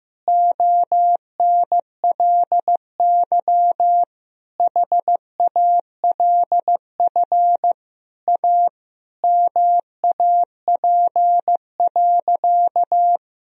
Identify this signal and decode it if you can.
Morse code, heard as a keyed tone, about 15 wpm, 700 Hz. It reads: ONLY HALF A MAP.
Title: Sentences from Top 1000 Words - Review Part 2 - 15wpm